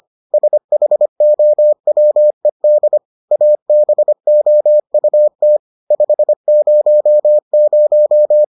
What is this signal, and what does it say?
Morse code, heard as a keyed tone, about 25 wpm, 600 Hz. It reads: SHOWED ABOUT 500